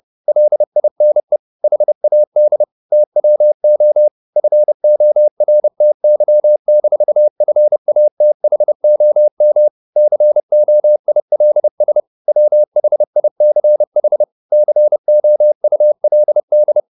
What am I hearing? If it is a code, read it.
Morse code, heard as a keyed tone, about 30 wpm, 600 Hz. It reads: LINE HAD TWO FORTY-FATHOM COILS WHICH COULD
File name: the old man and the sea / the old man and the sea 1 30wpm